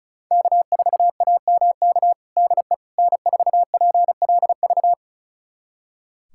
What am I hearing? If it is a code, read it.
Morse code, heard as a keyed tone, about 36 wpm, 700 Hz. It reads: K4AMK DE N4PLV
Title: USA Callsign DE Callsign - 5 Characters - 35wpm